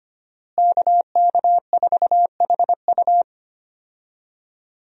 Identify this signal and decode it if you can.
Morse code, heard as a keyed tone, about 25 wpm, 700 Hz. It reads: KK4HU
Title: USA Call Signs - 5 Characters - Repeated 3x - 25wpm